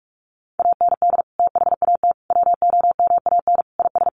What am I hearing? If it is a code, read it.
Morse code, heard as a keyed tone, about 44 wpm, 700 Hz. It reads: AND THAT WOMAN IS
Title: Sentences from Top 1100 Words 45wpm